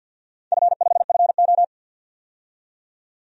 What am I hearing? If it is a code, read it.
Morse code, heard as a keyed tone, about 53 wpm, 700 Hz. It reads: F5FO
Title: DXCC-100 Call Signs - 4 Characters - 50wpm